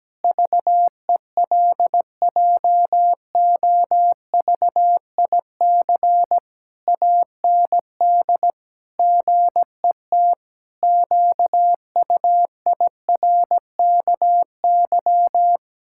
Morse code, heard as a keyed tone, about 17 wpm, 700 Hz. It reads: VELJOVIC AND GET QUIRKY
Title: Pangram Challenge v2 - 17wpm